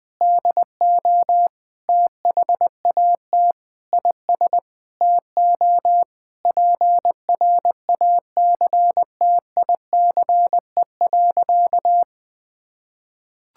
Morse code, heard as a keyed tone, about 21 wpm, 700 Hz. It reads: DO THAT IS TO PRACTICE.
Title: Sentences with 9-Word Phrases - 20wpm